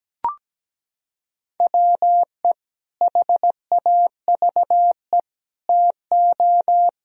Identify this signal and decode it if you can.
Morse code, heard as a keyed tone, about 17 wpm, 700 Hz. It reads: WE HAVE TO
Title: Sentences from Top 1000 Words - Review Part 3 - 17wpm